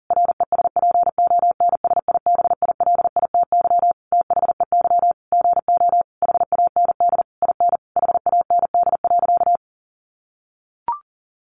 Morse code, heard as a keyed tone, about 40 wpm, 700 Hz. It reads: RESPONSIBILITY THEY GO HAND IN HAND.
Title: Sentences from Top 1200 Words 40wpm